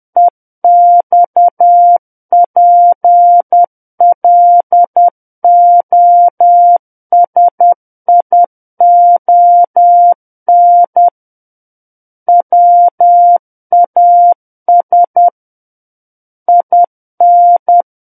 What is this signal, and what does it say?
Morse code, heard as a keyed tone, about 10 wpm, 700 Hz. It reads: EXPLOSION WAS IN
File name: Book One / 01-Chapter 1 - 10 WPM